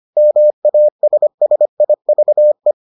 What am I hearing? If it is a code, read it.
Morse code, heard as a keyed tone, about 25 wpm, 600 Hz. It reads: MASSIVE